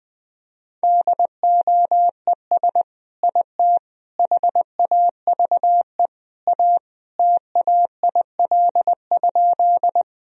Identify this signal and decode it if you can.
Morse code, heard as a keyed tone, about 20 wpm, 700 Hz. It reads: DOES IT HAVE A TAIL?
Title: Sentences from Top 1000 Words - Review Part 3 - 20wpm